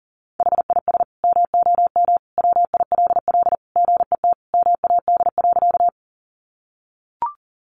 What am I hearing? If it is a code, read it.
Morse code, heard as a keyed tone, about 37 wpm, 700 Hz. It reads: HIS MOM WILL GET MAD.
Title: Additional Easy Phrases 40wpm